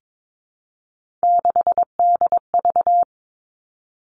Morse code, heard as a keyed tone, about 22 wpm, 700 Hz. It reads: 6DV
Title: All Cut Numbers - 3-Digit 22wpm